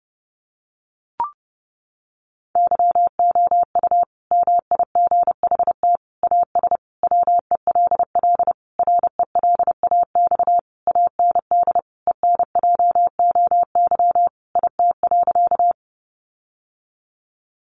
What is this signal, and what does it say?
Morse code, heard as a keyed tone, about 30 wpm, 700 Hz. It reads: YOU MIGHT AS WELL RELAX AND ENJOY IT.
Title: Sentences from Top 1400 Words 30wpm